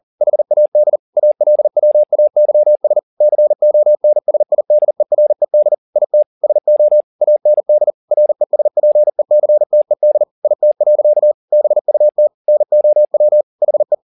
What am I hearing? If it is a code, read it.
Morse code, heard as a keyed tone, about 40 wpm, 600 Hz. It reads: HAD ALWAYS CONSIDERED IT SO AND RESPECTED IT. BUT NOW HE